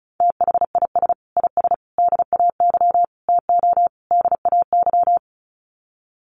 Morse code, heard as a keyed tone, about 34 wpm, 700 Hz. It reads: THIS IS DAY TO DAY
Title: Sets of 5 Words Encore 35wpm